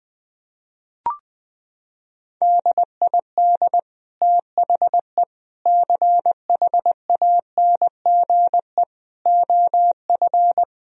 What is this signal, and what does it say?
Morse code, heard as a keyed tone, about 20 wpm, 700 Hz. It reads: DID THE CHANGE OF
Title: Sentences from Top 1000 Words - Review Part 4 - 20wpm